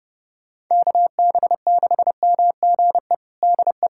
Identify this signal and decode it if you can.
Morse code, heard as a keyed tone, about 30 wpm, 700 Hz. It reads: KB6MGE DE
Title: USA Callsign DE Callsign 30wpm